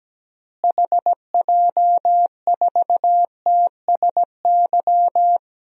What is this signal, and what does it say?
Morse code, heard as a keyed tone, about 17 wpm, 700 Hz. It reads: HJ4TSY
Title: DXCC-100 Call Signs - 6 Characters - Repeated 3x - 17wpm